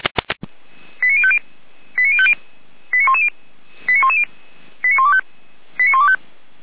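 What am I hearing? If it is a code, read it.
Five-tone selective call (ZVEI), heard as quick runs of five tones, tones of about 70 ms. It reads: 89050 89050 89110 89110 89125 89125
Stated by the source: ZVEI five-tone selective call received on SDR, fire brigade alerting.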